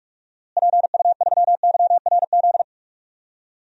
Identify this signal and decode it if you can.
Morse code, heard as a keyed tone, about 46 wpm, 700 Hz. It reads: PU3YRZ
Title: DXCC-100 Call Signs - 6 Characters - Repeated 3x - 45wpm